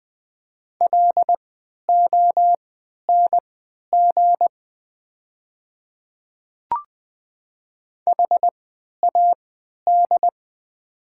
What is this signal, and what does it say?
Morse code, heard as a keyed tone, about 20 wpm, spaced as for 11 wpm, 700 Hz. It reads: LONG HAD
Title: 3x Character Spacing - Top 100 Words - 20wpm